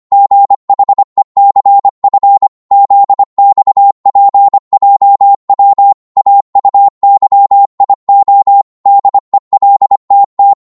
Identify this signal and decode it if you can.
Morse code, heard as a keyed tone, about 25 wpm, 800 Hz. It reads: GHECF ZXPJW AUYIO DELTT
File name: Morse / ltr-25wpm